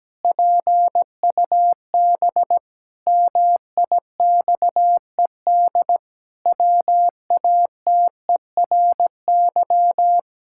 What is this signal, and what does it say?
Morse code, heard as a keyed tone, about 17 wpm, 700 Hz. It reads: PUB MIXED WATERY
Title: Pangram Challenge v2 - 17wpm